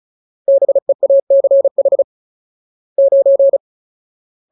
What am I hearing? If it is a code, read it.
Morse code, heard as a keyed tone, about 34 wpm, 550 Hz. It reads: BEACH 9
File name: Ultimate Word List / 35WPM Ultimate